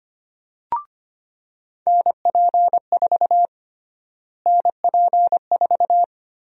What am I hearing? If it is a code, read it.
Morse code, heard as a keyed tone, about 25 wpm, 700 Hz. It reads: NP4 NP4
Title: USA Call Sign Prefixes - Repeated 2x - 25wpm